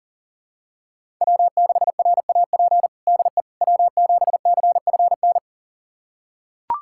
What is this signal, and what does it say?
Morse code, heard as a keyed tone, about 40 wpm, 700 Hz. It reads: W6RAP DE W7CFN
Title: USA Callsign DE Callsign - 5 Characters - 40wpm